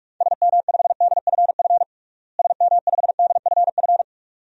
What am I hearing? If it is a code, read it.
Morse code, heard as a keyed tone, about 45 wpm, 700 Hz. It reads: SM5DFF SM5DFF
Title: DXCC-100 Call Signs - 6 Characters - Repeated 2x - 45wpm